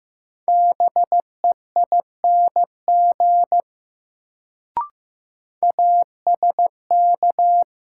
Morse code, heard as a keyed tone, about 15 wpm, 700 Hz. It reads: BEING ASK
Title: Sets of 4 Words - New Word Drill - 15wpm